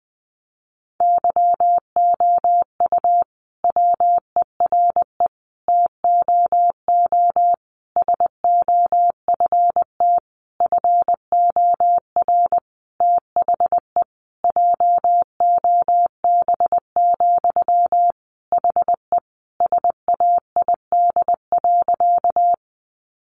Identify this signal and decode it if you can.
Morse code, heard as a keyed tone, about 20 wpm, 700 Hz. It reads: YOU WERE TOO SOFT FOR THE JOB, HE SAID.
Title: Sentences from Top 1500 Words 20wpm